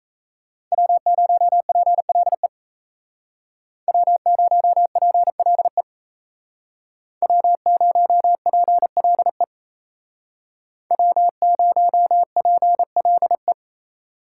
Morse code, heard as a keyed tone, about 36 wpm, 700 Hz. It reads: W0PLE W0PLE W0PLE W0PLE